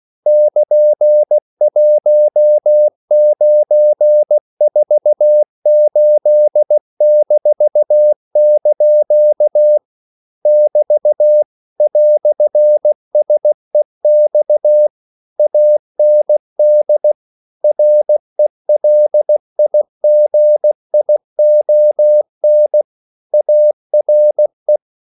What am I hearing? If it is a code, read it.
Morse code, heard as a keyed tone, about 16 wpm, 600 Hz. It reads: <KN>1948-) <BT> "SEX AND RELIGION ARE
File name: CW / Practice / EnglishQuotations0116